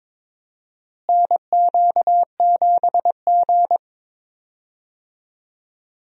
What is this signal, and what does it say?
Morse code, heard as a keyed tone, about 22 wpm, 700 Hz. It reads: NQ7G
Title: USA Call Signs - 4 Characters - Repeated 3x - 22wpm